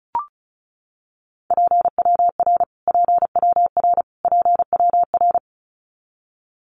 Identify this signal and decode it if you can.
Morse code, heard as a keyed tone, about 34 wpm, 700 Hz. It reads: PWR PWR PWR
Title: Abbreviations Repeated 3x - 35wpm